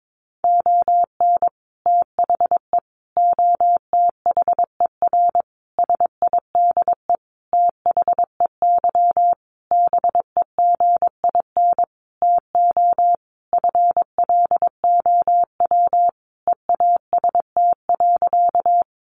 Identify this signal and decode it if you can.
Morse code, heard as a keyed tone, about 22 wpm, 700 Hz. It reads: ON THE OTHER SIDE THEY BEGIN TO FLOW EAST.
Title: Sentences from Top 1200 Words 22wpm